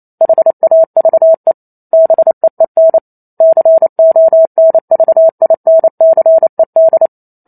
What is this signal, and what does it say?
Morse code, heard as a keyed tone, about 28 wpm, 650 Hz. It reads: HAVE BEEN CONVINCED
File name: Chapter 2 - 28 WPM